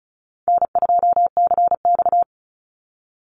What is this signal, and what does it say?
Morse code, heard as a keyed tone, about 34 wpm, 700 Hz. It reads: N2CX
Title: Easy Call Signs Repeated 2x - 35wpm